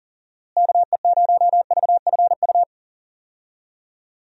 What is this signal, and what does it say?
Morse code, heard as a keyed tone, about 40 wpm, 700 Hz. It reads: KE0VFU
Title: USA Call Signs - 6 Characters - Repeated 2x - 40wpm